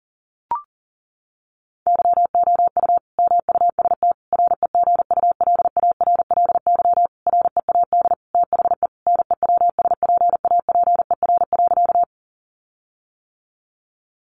Morse code, heard as a keyed tone, about 40 wpm, 700 Hz. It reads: YOU MUST REGULARLY READ THE NEWSPAPER.
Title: Sentences from Top 1400 Words 40wpm